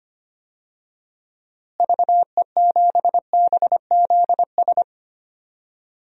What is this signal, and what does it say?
Morse code, heard as a keyed tone, about 25 wpm, 700 Hz. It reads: VE7BZS
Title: DXCC-100 Call Signs - 6 Characters - Repeated 2x - 25wpm